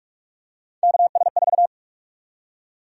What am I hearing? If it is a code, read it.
Morse code, heard as a keyed tone, about 46 wpm, 700 Hz. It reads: KS4
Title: USA Call Sign Prefixes - Repeated 2x - 45wpm